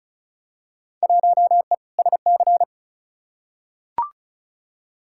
Morse code, heard as a keyed tone, about 36 wpm, 700 Hz. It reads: 1E SC
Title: Field Day Exchanges - 35wpm